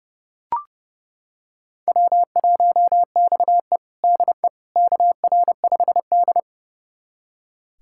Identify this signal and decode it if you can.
Morse code, heard as a keyed tone, about 30 wpm, 700 Hz. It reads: W1XE DE KR5D
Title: USA Callsign DE Callsign - 4 Characters - 30wpm